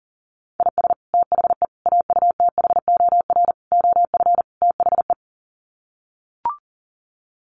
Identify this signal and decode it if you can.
Morse code, heard as a keyed tone, about 40 wpm, 700 Hz. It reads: IS THE AUTHOR OF THE
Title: Sets of 5 Words Encore 40wpm